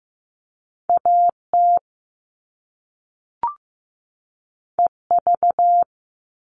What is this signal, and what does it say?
Morse code, heard as a keyed tone, about 15 wpm, 700 Hz. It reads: AT EV
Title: All Cut Numbers - 2-Digit 15wpm